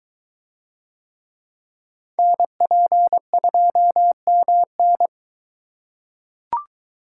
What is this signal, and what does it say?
Morse code, heard as a keyed tone, about 22 wpm, 700 Hz. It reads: NP2MN